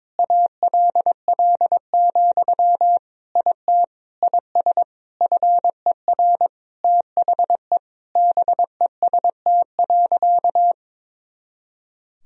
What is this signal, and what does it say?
Morse code, heard as a keyed tone, about 22 wpm, 700 Hz. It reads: ALL, IT IS FER THE BEST.
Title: Sentences from Top 500 Words and Abbreviations 22wpm